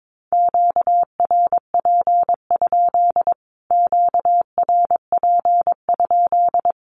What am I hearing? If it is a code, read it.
Morse code, heard as a keyed tone, about 22 wpm, 700 Hz. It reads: QRP? QRP?
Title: Q-Codes Repeated 3x 22wpm